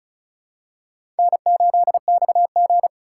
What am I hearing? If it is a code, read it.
Morse code, heard as a keyed tone, about 36 wpm, 700 Hz. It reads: N8XG
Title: USA Call Signs - 4 Characters - 2x Character Spacing - 35wpm